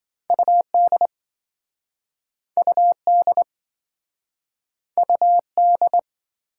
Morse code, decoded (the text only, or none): UD UD UD